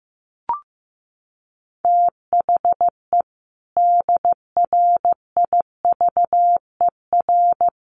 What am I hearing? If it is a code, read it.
Morse code, heard as a keyed tone, about 15 wpm, 700 Hz. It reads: THE DRIVER
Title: Sentences from Top 1000 Words 15wpm